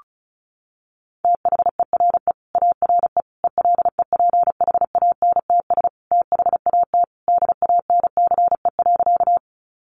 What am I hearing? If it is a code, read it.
Morse code, heard as a keyed tone, about 34 wpm, 700 Hz. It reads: THERE ARE ELEPHANTS THAT DANCE.
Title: Easy Phrases 35wpm